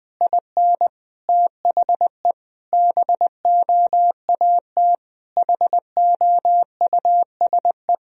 Morse code, heard as a keyed tone, about 20 wpm, 700 Hz. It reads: IN THE BOAT HOUSE
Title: Sentences from Top 1000 Words - Review Part 3 - 20wpm